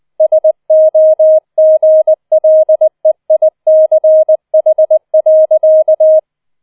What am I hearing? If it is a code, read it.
Morse code, heard as a keyed tone, about 20 wpm, 630 Hz. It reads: SOGLEICH.